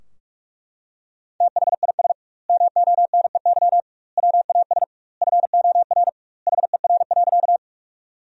Morse code, heard as a keyed tone, about 42 wpm, 700 Hz. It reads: THIS MONEY WAS FOR HER.
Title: 2x Word-Spacing - Sentences from Top 400 Words - 45wpm